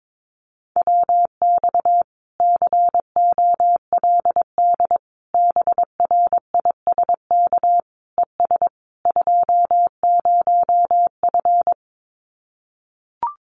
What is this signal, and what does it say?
Morse code, heard as a keyed tone, about 22 wpm, 700 Hz. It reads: WX COLD BRISK ES 20F
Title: Common QSO Phrases Encore 22wpm